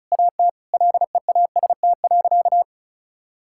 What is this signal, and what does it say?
Morse code, heard as a keyed tone, about 36 wpm, 700 Hz. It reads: AT LEAST.
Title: Sentences from Top 1000 Words - Review Part 5 - 35wpm